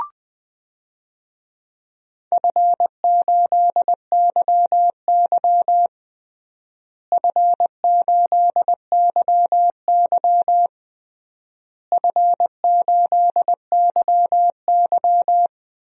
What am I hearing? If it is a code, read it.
Morse code, heard as a keyed tone, about 20 wpm, 700 Hz. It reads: F8YY F8YY F8YY
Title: DXCC-100 Call Signs - 4 Characters - Repeated 3x - 20wpm